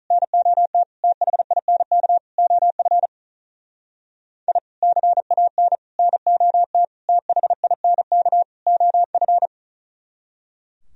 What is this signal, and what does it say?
Morse code, heard as a keyed tone, about 43 wpm, 700 Hz. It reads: NOT THINK OF I CAN NOT THINK OF
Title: Speed-Racing - Sets of 5 Words 35wpm